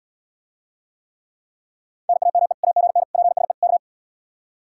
Morse code, heard as a keyed tone, about 73 wpm, 700 Hz. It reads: NAME AGN PSE K